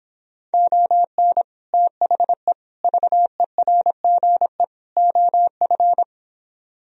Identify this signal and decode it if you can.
Morse code, heard as a keyed tone, about 26 wpm, 700 Hz. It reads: ON THE VERGE OF